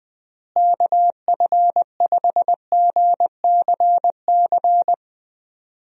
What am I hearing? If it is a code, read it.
Morse code, heard as a keyed tone, about 21 wpm, 700 Hz. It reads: KF5GCC
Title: USA Call Signs - 6 Characters - Repeated 2x - 20wpm